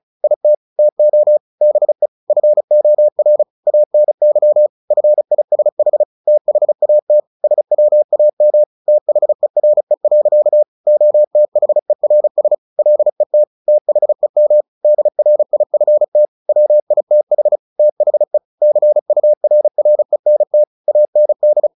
Morse code, heard as a keyed tone, about 34 wpm, 600 Hz. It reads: IT TO BE FOR ANY FISH THAT SWAM THERE. OTHERS LET THEM DRIFT WITH THE CURRENT AND
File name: the old man and the sea / the old man and the sea 1 35wpm